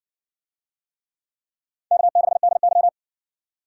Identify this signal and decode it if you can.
Morse code, heard as a keyed tone, about 56 wpm, 700 Hz. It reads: K6DY